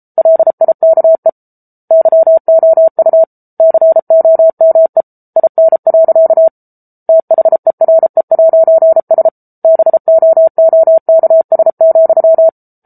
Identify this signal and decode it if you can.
Morse code, heard as a keyed tone, about 34 wpm, 650 Hz. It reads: LIKE YOU COME IN. THERE'S BOOKS,
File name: Book Two / 07-Chapter 7 - 33 WPM